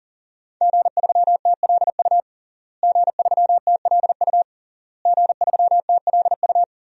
Morse code, heard as a keyed tone, about 40 wpm, 700 Hz. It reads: G3TLU G3TLU G3TLU
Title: DXCC-100 Call Signs - 5 Characters - Repeated 3x - 40wpm